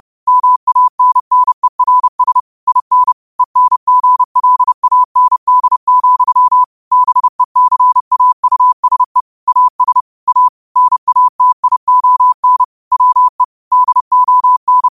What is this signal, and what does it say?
Morse code, heard as a keyed tone, about 30 wpm, 1000 Hz. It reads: MANNERS IN ENGLAND, BECAUSE AS A NATION WE DON
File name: qotdfiles / 30wpm